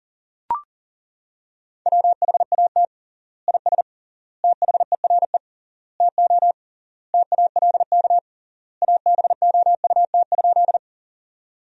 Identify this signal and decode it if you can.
Morse code, heard as a keyed tone, about 37 wpm, 700 Hz. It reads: WHAT IS THERE TO TALK ABOUT?
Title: 3x Word-Spacing - Sentences from Top 400 Words - 40wpm